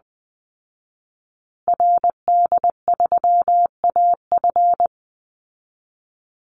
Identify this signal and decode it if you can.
Morse code, heard as a keyed tone, about 20 wpm, 700 Hz. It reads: RD3AF
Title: Call Signs Repeated 2x - 20wpm